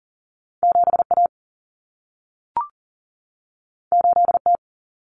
Morse code, heard as a keyed tone, about 40 wpm, 700 Hz. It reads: 7A 8T
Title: Common Cut Numbers - 2-Digit 40wpm